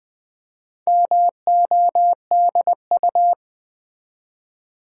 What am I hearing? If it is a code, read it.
Morse code, heard as a keyed tone, about 20 wpm, 700 Hz. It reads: MODU
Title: Build A Word - 2 Letters At A Time - 6-Letter Words - 20wpm